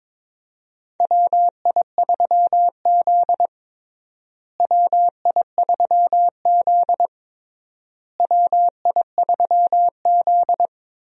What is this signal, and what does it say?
Morse code, heard as a keyed tone, about 22 wpm, 700 Hz. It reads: WI3Z WI3Z WI3Z
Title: USA Call Signs - 4 Characters - Repeated 3x - 22wpm